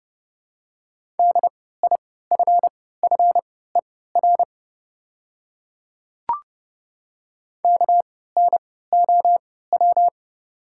Morse code, heard as a keyed tone, about 30 wpm, spaced as for 17 wpm, 700 Hz. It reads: DIFFER KNOW
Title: 3x Character Spacing - Top 300 Words - 30wpm